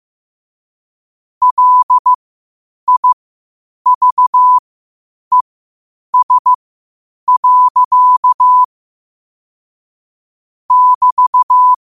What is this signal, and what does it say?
Morse code, heard as a keyed tone, about 15 wpm, spaced as for 8 wpm, 1000 Hz. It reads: LIVES. <BT>
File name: qotdfiles / 8wpm